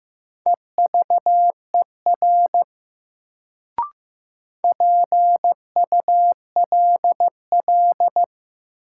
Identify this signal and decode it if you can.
Morse code, heard as a keyed tone, about 15 wpm, 700 Hz. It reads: EVER PULL
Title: Sentences from Top 400 Words - New Word Drill - 15wpm